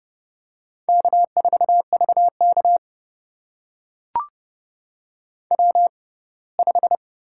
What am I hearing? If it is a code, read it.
Morse code, heard as a keyed tone, about 30 wpm, spaced as for 20 wpm, 700 Hz. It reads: K4VK W5
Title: USA Call Signs - 4 Characters - 6x Character Spacing - 30wpm